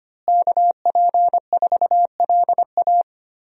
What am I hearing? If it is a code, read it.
Morse code, heard as a keyed tone, about 25 wpm, 700 Hz. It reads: KP4LA
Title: USA Call Signs - 5 Characters - Repeated 2x - 25wpm